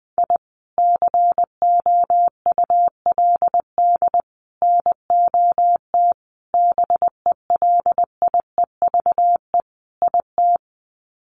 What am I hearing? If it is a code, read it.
Morse code, heard as a keyed tone, about 20 wpm, 700 Hz. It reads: I COULD NOT BELIEVE IT
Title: Sets of 5 Words Encore 20wpm